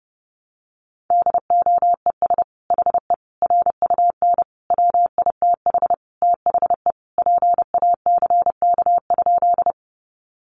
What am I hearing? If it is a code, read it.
Morse code, heard as a keyed tone, about 30 wpm, 700 Hz. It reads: DOES HE RUN WITH THE PACK?
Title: Sentences from Top 1300 Words 30wpm